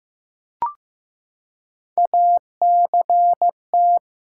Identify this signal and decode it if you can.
Morse code, heard as a keyed tone, about 15 wpm, 700 Hz. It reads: ACT
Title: Three Letter Words - 15wpm